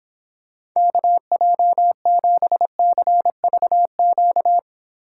Morse code, heard as a keyed tone, about 26 wpm, 700 Hz. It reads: KJ7CVQ